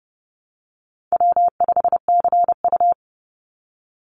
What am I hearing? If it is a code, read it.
Morse code, heard as a keyed tone, about 30 wpm, 700 Hz. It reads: W5CU
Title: Easy Call Signs - 30wpm